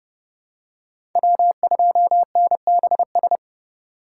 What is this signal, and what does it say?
Morse code, heard as a keyed tone, about 31 wpm, 700 Hz. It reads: W2NBS
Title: USA Call Signs - 5 Characters - 2x Character Spacing - 30wpm